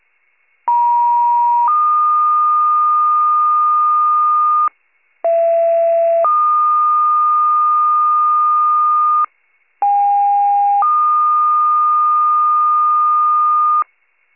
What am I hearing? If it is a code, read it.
Two-tone page, heard as a one-second tone, then a three-second tone, about 950 then 1220 Hz, about 670 then 1220 Hz, about 810 then 1220 Hz. It is heard from 0.7 to 13.8 s.
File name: wcalert